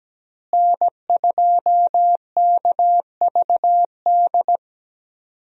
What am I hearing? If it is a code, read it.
Morse code, heard as a keyed tone, about 17 wpm, 700 Hz. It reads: N2KVD
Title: USA Call Signs - 5 Characters - Repeated 2x - 17wpm